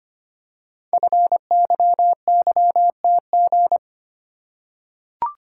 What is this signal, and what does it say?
Morse code, heard as a keyed tone, about 25 wpm, 700 Hz. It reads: FYYTG
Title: Code Groups - 5 Letters - 25wpm